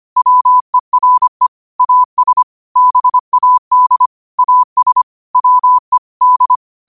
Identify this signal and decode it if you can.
Morse code, heard as a keyed tone, about 25 wpm, 1000 Hz. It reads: WERE AS BAD AS WE D